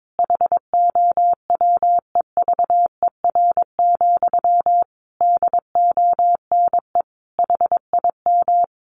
Morse code, heard as a keyed tone, about 22 wpm, 700 Hz. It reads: HOWEVER, DONE HIM
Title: Sentences from Top 1200 Words 22wpm